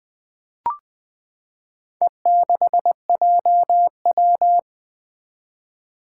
Morse code, heard as a keyed tone, about 20 wpm, 700 Hz. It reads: E6JW
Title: Four Challenging Letter-Number Combinations 20wpm